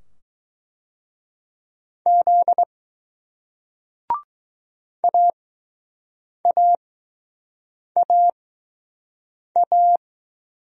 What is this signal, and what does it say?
Morse code, heard as a keyed tone, about 20 wpm, spaced as for 6 wpm, 700 Hz. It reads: Z AAAA